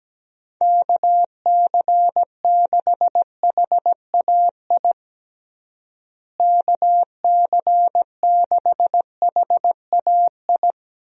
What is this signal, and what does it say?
Morse code, heard as a keyed tone, about 17 wpm, 700 Hz. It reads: KC6HAI KC6HAI
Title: USA Call Signs - 6 Characters - Repeated 3x - 17wpm